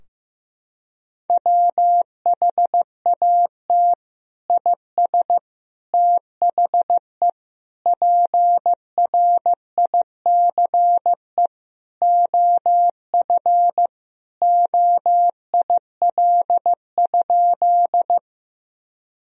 Morse code, heard as a keyed tone, about 15 wpm, 700 Hz. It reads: WHAT IS THE PRICE OF OIL?
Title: Sentences from Top 1000 Words - Review Part 4 - 15wpm